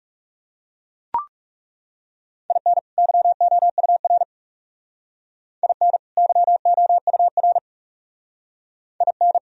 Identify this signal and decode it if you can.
Morse code, heard as a keyed tone, about 39 wpm, 700 Hz. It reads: IN YOUR IN YOUR IN